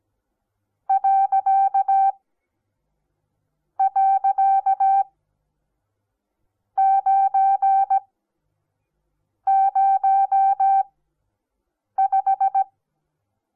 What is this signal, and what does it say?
Morse code, heard as a keyed tone, about 17 wpm, spaced as for 5 wpm, 780 Hz. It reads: ..905
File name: Onlinelehrgang / morsen / DJ4UF Morsen08